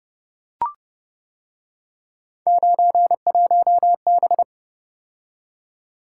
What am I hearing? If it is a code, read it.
Morse code, heard as a keyed tone, about 30 wpm, 700 Hz. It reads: 91B